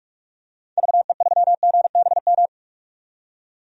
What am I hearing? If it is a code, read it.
Morse code, heard as a keyed tone, about 47 wpm, 700 Hz. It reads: VE3GBM
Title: DXCC-100 Call Signs - 6 Characters - 45wpm